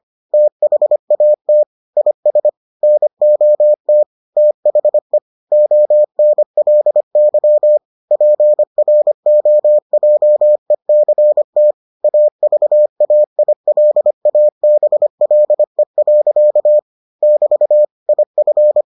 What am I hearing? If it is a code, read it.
Morse code, heard as a keyed tone, about 25 wpm, 600 Hz. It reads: THAT IS NOT THE ONLY PROJECT AVAILABLE. <BT> IF